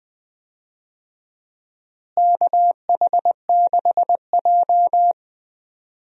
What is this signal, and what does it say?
Morse code, heard as a keyed tone, about 20 wpm, 700 Hz. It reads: KH6J
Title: USA Call Signs - 4 Characters - Repeated 3x - 20wpm